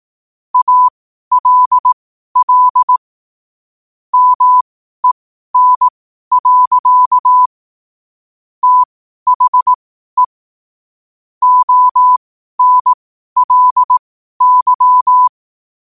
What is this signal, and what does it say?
Morse code, heard as a keyed tone, about 18 wpm, spaced as for 12 wpm, 1000 Hz. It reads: ALL MEN. THE ONLY
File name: qotdfiles / 12wpm